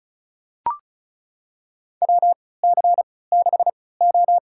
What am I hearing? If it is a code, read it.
Morse code, heard as a keyed tone, about 34 wpm, spaced as for 20 wpm, 700 Hz. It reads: WC6O
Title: USA Call Signs - 3x Character Spacing - 35wpm